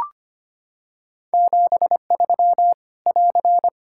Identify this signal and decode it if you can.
Morse code, heard as a keyed tone, about 25 wpm, 700 Hz. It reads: 73 <AR>
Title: K1USN SST QSO Practice 25wpm